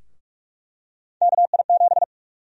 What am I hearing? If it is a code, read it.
Morse code, heard as a keyed tone, about 44 wpm, 700 Hz. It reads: KI7
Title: USA Call Sign Prefixes - 45wpm